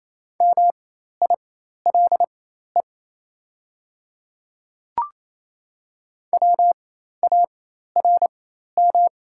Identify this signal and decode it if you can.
Morse code, heard as a keyed tone, about 28 wpm, spaced as for 13 wpm, 700 Hz. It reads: MILE WARM
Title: 4x Character Spacing - Top 500 Words - 28wpm